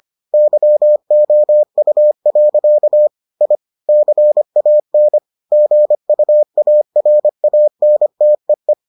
Morse code, heard as a keyed tone, about 25 wpm, 600 Hz. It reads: YOU. I CAN GUARANTEE